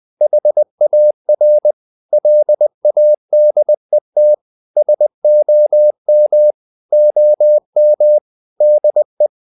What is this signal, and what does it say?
Morse code, heard as a keyed tone, about 20 wpm, 600 Hz. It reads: HAR LADET SOM OM DE
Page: Ferie 20 wpm